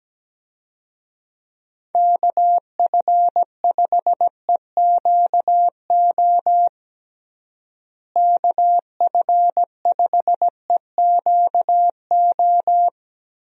Morse code, heard as a keyed tone, about 17 wpm, 700 Hz. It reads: KF5EQO KF5EQO
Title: USA Call Signs - 6 Characters - Repeated 2x - 17wpm